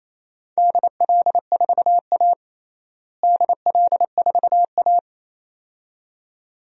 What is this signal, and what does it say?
Morse code, heard as a keyed tone, about 28 wpm, 700 Hz. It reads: DL4A DL4A
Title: DXCC-100 Call Signs - 4 Characters - Repeated 2x - 28wpm